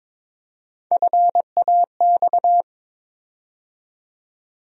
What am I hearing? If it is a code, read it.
Morse code, heard as a keyed tone, about 22 wpm, 700 Hz. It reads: FAX
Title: USA Call Sign Postfixes - 3 Letters 22wpm